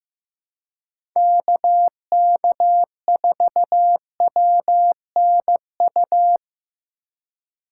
Morse code, heard as a keyed tone, about 18 wpm, 700 Hz. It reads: KK4WNU
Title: Speed Racing - USA 2x3 Call Signs 15wpm